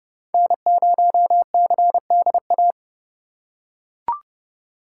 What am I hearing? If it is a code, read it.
Morse code, heard as a keyed tone, about 29 wpm, 700 Hz. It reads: N0CDA